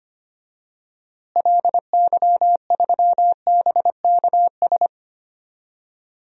Morse code, heard as a keyed tone, about 25 wpm, 700 Hz. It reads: LY3BKS
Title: DXCC-100 Call Signs - 6 Characters - Repeated 3x - 25wpm